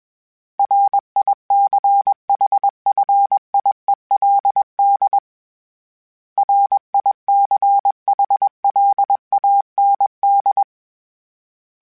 Morse code, heard as a keyed tone, about 21 wpm, 800 Hz. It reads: RICHFIELD RICHLAND
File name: fromSMs / FromSMs mp3s / Cities@20